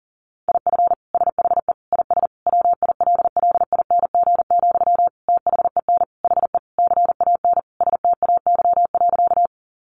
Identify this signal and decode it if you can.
Morse code, heard as a keyed tone, about 40 wpm, 700 Hz. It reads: IF SHE IS WILLING, THEN HE CAN STAY.
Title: Sentences from Top 1000 Words 40wpm